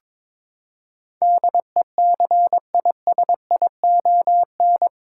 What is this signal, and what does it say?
Morse code, heard as a keyed tone, about 22 wpm, 700 Hz. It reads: DECISION
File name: practice / 22wpm / Eight Letter Words - 22wpm